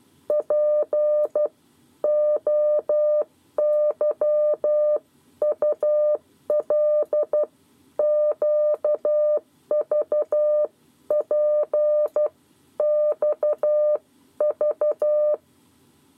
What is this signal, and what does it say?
Morse code, heard as a keyed tone, about 12 wpm, spaced as for 10 wpm, 600 Hz. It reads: POYUL QVPXV
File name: 7 Lezione cw 2